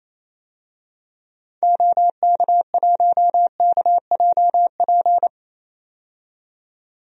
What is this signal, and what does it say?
Morse code, heard as a keyed tone, about 28 wpm, 700 Hz. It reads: OK1KJP
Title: DXCC-100 Call Signs - 6 Characters - 28wpm